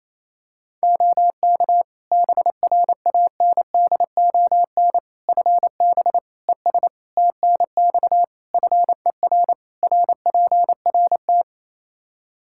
Morse code, heard as a keyed tone, about 28 wpm, 700 Hz. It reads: OK BRANDON FB ES TNX FER RPRT
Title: Practice QSO - Exchange 2 - Info - 28wpm